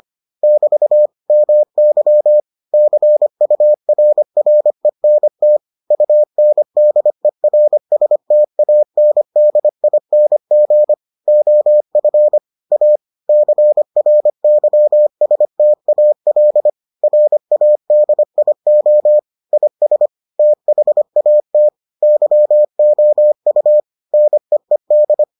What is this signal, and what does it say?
Morse code, heard as a keyed tone, about 25 wpm, 600 Hz. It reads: <BT> MY CURRENT UNDERSTANDING OF A CRYSTAL RADIO IS THAT YOU NEED